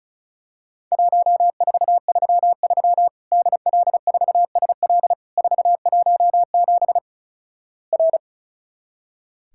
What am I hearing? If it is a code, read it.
Morse code, heard as a keyed tone, about 36 wpm, 700 Hz. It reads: 1433 DL4SL 417 R
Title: Worked All Europe DX Contest - QTC Exchange - 35wpm